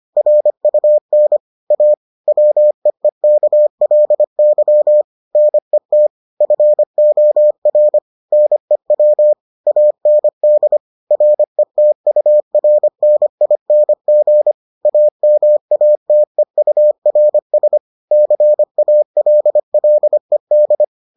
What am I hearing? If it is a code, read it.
Morse code, heard as a keyed tone, about 25 wpm, 600 Hz. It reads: RUN A WEEKLY NET FOR NEW AND RETURNING AMATEURS CALLED